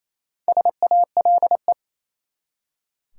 Morse code, heard as a keyed tone, about 28 wpm, 700 Hz. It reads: SALE
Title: Four Letter Words - 28wpm